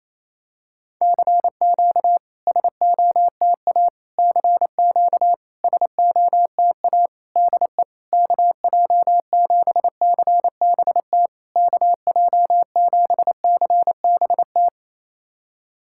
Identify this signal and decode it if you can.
Morse code, heard as a keyed tone, about 28 wpm, 700 Hz. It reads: CQ SOTA CQ SOTA DE KJ7CBT KJ7CBT
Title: SOTA QSO Practice 28wpm